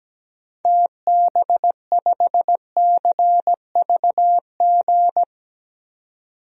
Morse code, heard as a keyed tone, about 17 wpm, 700 Hz. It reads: TB5CVG
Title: DXCC-100 Call Signs - 6 Characters - 17wpm